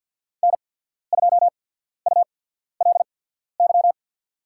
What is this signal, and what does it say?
Morse code, heard as a keyed tone, about 53 wpm, spaced as for 14 wpm, 700 Hz. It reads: N2ULY
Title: USA Call Signs - 5 Characters - 8x Character Spacing - 50wpm